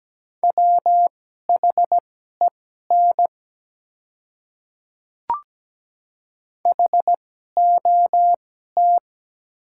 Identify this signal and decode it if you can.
Morse code, heard as a keyed tone, about 17 wpm, spaced as for 12 wpm, 700 Hz. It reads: WHEN HOT
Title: 2x Character Spacing - Top 100 Words - 17wpm